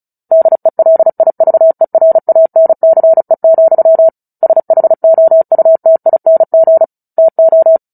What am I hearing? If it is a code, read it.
Morse code, heard as a keyed tone, about 34 wpm, 650 Hz. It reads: DELIVERANCE, SHOUTING TO
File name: Book Two / 09-Chapter 9 - 35 WPM